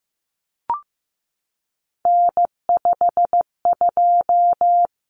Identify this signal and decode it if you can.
Morse code, heard as a keyed tone, about 15 wpm, 700 Hz. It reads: N52
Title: Common Cut Numbers - 3-Digit 15wpm